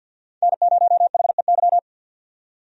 Morse code, heard as a keyed tone, about 52 wpm, 700 Hz. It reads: N0HEY
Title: USA Call Signs - 5 Characters - 50wpm